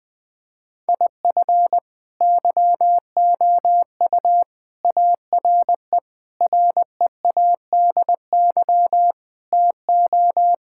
Morse code, heard as a keyed tone, about 21 wpm, 700 Hz. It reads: IF YOU ARE READY TO
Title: Sentences with 6-Word Phrases - 20wpm